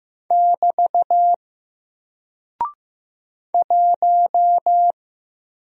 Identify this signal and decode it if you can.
Morse code, heard as a keyed tone, about 15 wpm, 700 Hz. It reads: <BT> 1
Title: Single Character (Letter:Number:Punctuation:Prosign) 15wpm